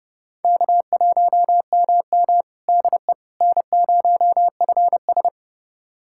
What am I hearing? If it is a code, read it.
Morse code, heard as a keyed tone, about 30 wpm, 700 Hz. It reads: K1MM DE N0FS
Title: USA Callsign DE Callsign - 4 Characters - 30wpm